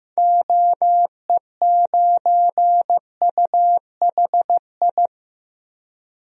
Morse code, heard as a keyed tone, about 15 wpm, 700 Hz. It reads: OE9UHI
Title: DXCC-100 Call Signs - 6 Characters - Repeated 2x - 15wpm